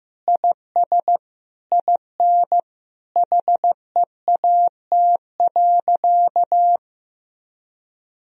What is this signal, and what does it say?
Morse code, heard as a keyed tone, about 16 wpm, 700 Hz. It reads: IS IN HEAT.
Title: Sentences from Top 1000 Words - Review Part 3 - 15wpm